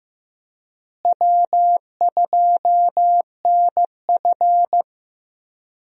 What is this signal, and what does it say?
Morse code, heard as a keyed tone, about 15 wpm, 700 Hz. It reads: W2NF
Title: USA Call Signs - 4 Characters - Repeated 3x - 15wpm